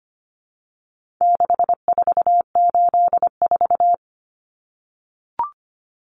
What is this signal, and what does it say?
Morse code, heard as a keyed tone, about 25 wpm, 700 Hz. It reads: 6484
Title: Numbers - 4-Digit 25wpm